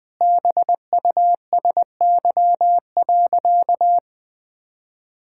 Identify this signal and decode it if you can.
Morse code, heard as a keyed tone, about 20 wpm, 700 Hz. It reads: BUSY.
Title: Sentences with 7-Word Phrases - 20wpm